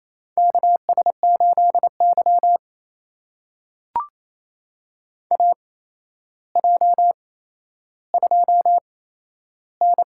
Morse code, heard as a keyed tone, about 28 wpm, spaced as for 12 wpm, 700 Hz. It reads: KS8Y AJ2N
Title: USA Call Signs - 4 Characters - 8x Character Spacing - 28wpm